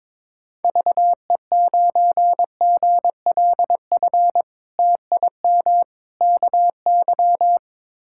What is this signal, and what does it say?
Morse code, heard as a keyed tone, about 22 wpm, 700 Hz. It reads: VE9GLF TIM KY
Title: K1USN SST QSO Practice - 22wpm